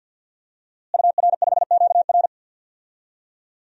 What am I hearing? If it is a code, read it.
Morse code, heard as a keyed tone, about 52 wpm, 700 Hz. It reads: UR5QR
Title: DXCC-100 Call Signs - 5 Characters - Repeated 3x - 50wpm